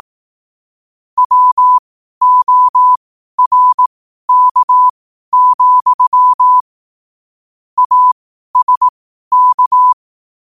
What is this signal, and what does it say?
Morse code, heard as a keyed tone, about 18 wpm, spaced as for 12 wpm, 1000 Hz. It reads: WORK, ASK
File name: qotdfiles / 12wpm